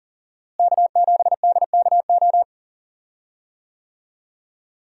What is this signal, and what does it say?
Morse code, heard as a keyed tone, about 40 wpm, 700 Hz. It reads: K7DKO
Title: USA Call Signs - 5 Characters - Repeated 2x - 40wpm